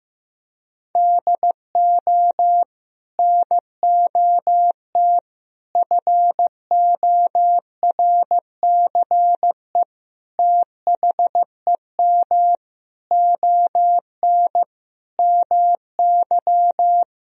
Morse code, heard as a keyed tone, about 15 wpm, 700 Hz. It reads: DO NOT FORCE THEM ON MY
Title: Sentences from Top 1000 Words - Review Part 3 - 15wpm